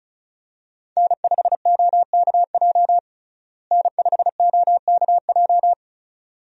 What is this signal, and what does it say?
Morse code, heard as a keyed tone, about 36 wpm, 700 Hz. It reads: N5OKJ N5OKJ
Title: USA Call Signs - 5 Characters - Repeated 2x - 35wpm